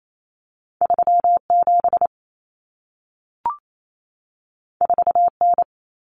Morse code, heard as a keyed tone, about 28 wpm, 700 Hz. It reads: 37 4N
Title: Common Cut Numbers - 2-Digit 28wpm